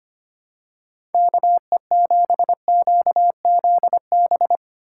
Morse code, heard as a keyed tone, about 25 wpm, 700 Hz. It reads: KE7QZB
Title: USA Call Signs - 6 Characters - 2x Character Spacing - 25wpm